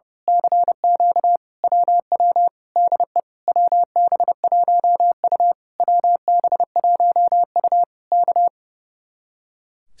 Morse code, heard as a keyed tone, about 28 wpm, 700 Hz. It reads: CQ WW DE WB1U WB1U K
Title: Speed-Racing Common QSO Phrases Encore - 30wpm